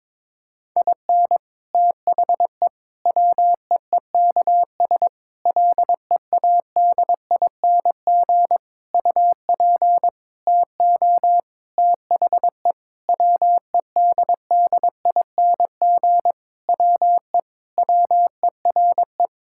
Morse code, heard as a keyed tone, about 22 wpm, 700 Hz. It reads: IN THE WEEKS LEADING UP TO THE WEDDING WE WERE
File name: Sentences with 7-Word Phrases - 22wpm